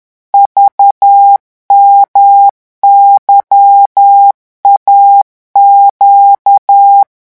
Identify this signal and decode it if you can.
Morse code, heard as a keyed tone, about 10 wpm, 800 Hz. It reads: VMYAQ
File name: morse mp3 / revision2-vitesse 10 mots